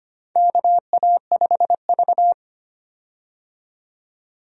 Morse code, heard as a keyed tone, about 25 wpm, 700 Hz. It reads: KA5V
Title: USA Call Signs - 4 Characters - 25wpm